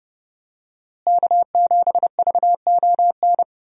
Morse code, heard as a keyed tone, about 30 wpm, 700 Hz. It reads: K7VON
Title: USA Call Signs - 5 Characters - Repeated 2x - 30wpm